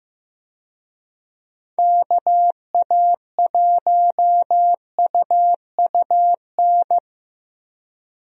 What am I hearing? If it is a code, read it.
Morse code, heard as a keyed tone, about 15 wpm, 700 Hz. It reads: KA1UUN
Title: USA Call Signs - 6 Characters - Repeated 2x - 15wpm